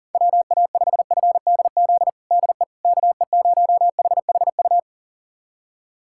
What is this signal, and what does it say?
Morse code, heard as a keyed tone, about 40 wpm, 700 Hz. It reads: WA5FDZ DE KE0HHU
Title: USA Callsign DE Callsign 40wpm